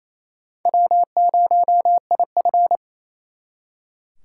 Morse code, heard as a keyed tone, about 36 wpm, 700 Hz. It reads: W0IF